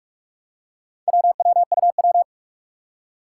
Morse code, heard as a keyed tone, about 46 wpm, 700 Hz. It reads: WWUW
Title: Four Challenging Letter-Number Combinations 45wpm